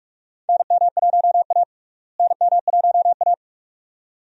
Morse code, heard as a keyed tone, about 46 wpm, 700 Hz. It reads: NM1A NM1A
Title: USA Call Signs - 4 Characters - Repeated 2x - 45wpm